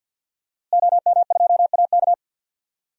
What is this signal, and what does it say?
Morse code, heard as a keyed tone, about 50 wpm, 700 Hz. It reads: OM2AK
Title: DXCC-100 Call Signs - 5 Characters - Repeated 2x - 50wpm